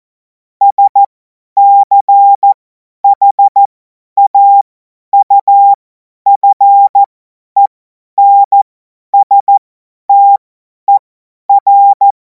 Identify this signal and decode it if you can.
Morse code, heard as a keyed tone, about 14 wpm, spaced as for 10 wpm, 800 Hz. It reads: SCHAUFENSTER
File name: Morsekurs / Witze / Witz 17. März 2026 50 F70